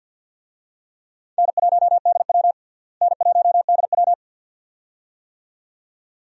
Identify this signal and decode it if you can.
Morse code, heard as a keyed tone, about 52 wpm, 700 Hz. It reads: N1DW N1DW
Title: USA Call Signs - 4 Characters - Repeated 2x - 50wpm